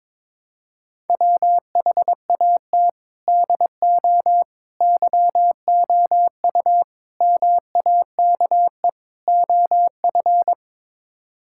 Morse code, heard as a keyed tone, about 22 wpm, 700 Hz. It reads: WHAT DO YOU MAKE OF
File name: practice / 22wpm / Build A Phrase - 5-Word Phrases - 22wpm